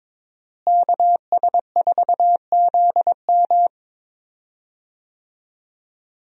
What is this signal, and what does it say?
Morse code, heard as a keyed tone, about 21 wpm, 700 Hz. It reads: KS4ZM
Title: USA Call Signs - 5 Characters - Repeated 2x - 22wpm